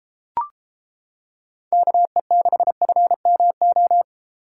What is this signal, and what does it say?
Morse code, heard as a keyed tone, about 31 wpm, 700 Hz. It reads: KE6FMO